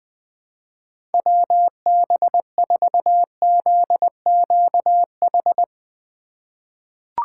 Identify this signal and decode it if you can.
Morse code, heard as a keyed tone, about 20 wpm, 700 Hz. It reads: WB4ZQH
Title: USA Call Signs - 6 Characters - 2x Character Spacing - 20wpm